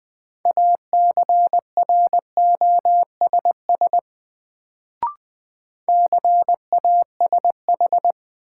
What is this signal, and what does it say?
Morse code, heard as a keyed tone, about 20 wpm, 700 Hz. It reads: ACROSS CASH
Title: Sentences from Top 700 Words - New Word Drill - 20wpm